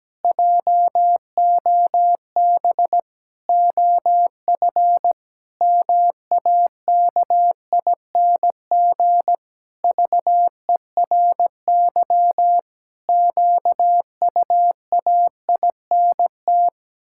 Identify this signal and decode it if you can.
Morse code, heard as a keyed tone, about 17 wpm, 700 Hz. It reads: JOB OF MAKING VERY QUAINT
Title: Pangram Challenge v2 - 17wpm